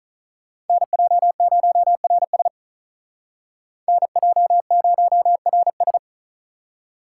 Morse code, heard as a keyed tone, about 43 wpm, 700 Hz. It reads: NJ0RS NJ0RS
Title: Speed Racing - USA 2x2 Call Signs 35wpm